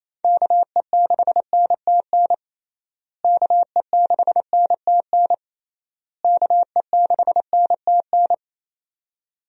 Morse code, heard as a keyed tone, about 28 wpm, 700 Hz. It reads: KE6NTN KE6NTN KE6NTN
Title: USA Call Signs - 6 Characters - Repeated 3x - 28wpm